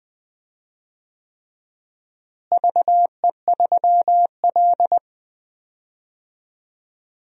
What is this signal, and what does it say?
Morse code, heard as a keyed tone, about 20 wpm, 700 Hz. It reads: VE3L
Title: DXCC-100 Call Signs - 4 Characters - Repeated 3x - 20wpm